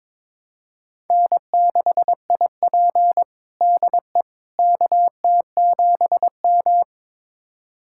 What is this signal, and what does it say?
Morse code, heard as a keyed tone, about 21 wpm, 700 Hz. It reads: N6IP DE KT7M
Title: USA Callsign DE Callsign - 4 Characters - 22wpm